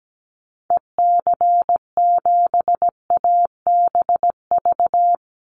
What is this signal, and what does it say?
Morse code, heard as a keyed tone, about 17 wpm, 700 Hz. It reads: EC7ABV
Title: Call Signs - 17wpm